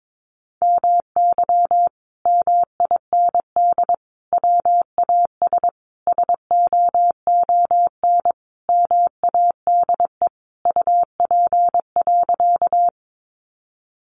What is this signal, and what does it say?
Morse code, heard as a keyed tone, about 22 wpm, 700 Hz. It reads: MY MIND WAS SOON MADE UP.
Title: Sentences from Top 500 Words 22wpm